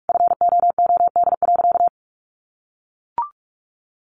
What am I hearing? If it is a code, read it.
Morse code, heard as a keyed tone, about 44 wpm, 700 Hz. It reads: FOOD.
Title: Sentences from Top 1100 Words 45wpm